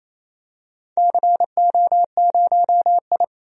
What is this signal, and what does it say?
Morse code, heard as a keyed tone, about 28 wpm, 700 Hz. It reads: CO0I
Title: DXCC-100 Call Signs - 4 Characters - 28wpm